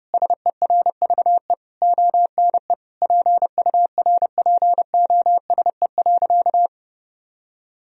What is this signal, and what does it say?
Morse code, heard as a keyed tone, about 30 wpm, 700 Hz. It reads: SERVE ONE PURPOSE.
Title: Sentences from Top 1000 Words - Review Part 2 - 30wpm